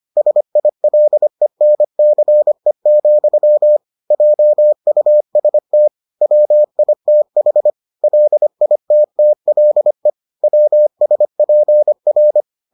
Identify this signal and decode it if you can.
Morse code, heard as a keyed tone, about 25 wpm, 600 Hz. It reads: SILENCE, JUST WITH LITTLE WSPR